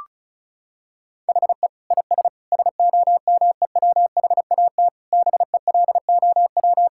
Morse code, heard as a keyed tone, about 36 wpm, 700 Hz. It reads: HE IS SOMEWHAT BELOW
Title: Sentences from Top 1600 Words 35wpm